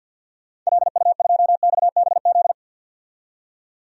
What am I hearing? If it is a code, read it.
Morse code, heard as a keyed tone, about 52 wpm, 700 Hz. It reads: LU2XBZ
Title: DXCC-100 Call Signs - 6 Characters - 50wpm